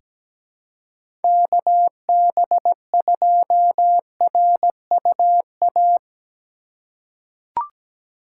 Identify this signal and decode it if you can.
Morse code, heard as a keyed tone, about 17 wpm, 700 Hz. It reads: KB2RUA
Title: USA Call Signs - 6 Characters - 4x Character Spacing - 17wpm